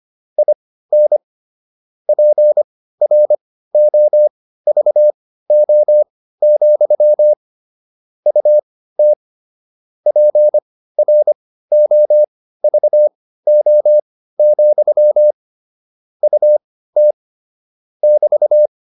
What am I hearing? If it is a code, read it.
Morse code, heard as a keyed tone, about 25 wpm, spaced as for 15 wpm, 600 Hz. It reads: IN PROVO, UT PROVO, UT <BT>